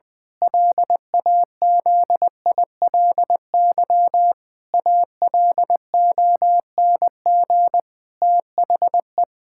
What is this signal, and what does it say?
Morse code, heard as a keyed tone, about 20 wpm, 700 Hz. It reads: LAZILY ALONG THE
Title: Pangram Challenge v2 - 20wpm